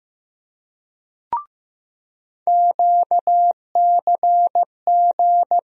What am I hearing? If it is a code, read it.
Morse code, heard as a keyed tone, about 15 wpm, 700 Hz. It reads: QCG
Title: USA Call Sign Postfixes - 3 Letters 15wpm